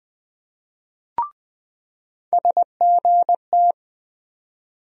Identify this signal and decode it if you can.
Morse code, heard as a keyed tone, about 20 wpm, 700 Hz. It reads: SGT
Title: USA Call Sign Postfixes - 3 Letters 20wpm